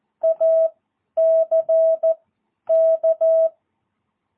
Morse code, heard as a keyed tone, about 14 wpm, spaced as for 10 wpm, 650 Hz. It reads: ACK
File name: jh8cbh / 2022cw